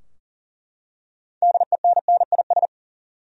Morse code, heard as a keyed tone, about 40 wpm, 700 Hz. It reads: DENNIS
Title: USA Common Names - 40wpm